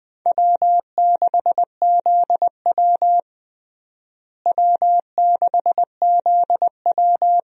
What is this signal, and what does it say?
Morse code, heard as a keyed tone, about 20 wpm, 700 Hz. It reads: W6ZW W6ZW
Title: USA Call Signs - 4 Characters - Repeated 2x - 20wpm